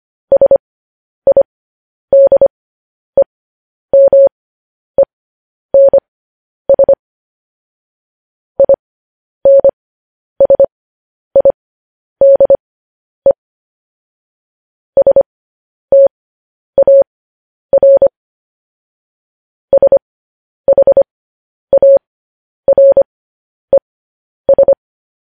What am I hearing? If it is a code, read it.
Morse code, heard as a keyed tone, about 25 wpm, spaced as for 10 wpm, 550 Hz. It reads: SIDEMENS INSIDE STAR SHARES
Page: news-headlines-in-morse-code-at-10-wpm